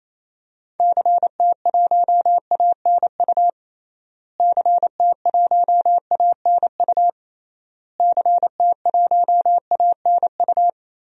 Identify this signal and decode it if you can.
Morse code, heard as a keyed tone, about 28 wpm, 700 Hz. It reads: CT1ANU CT1ANU CT1ANU
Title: DXCC-100 Call Signs - 6 Characters - Repeated 3x - 28wpm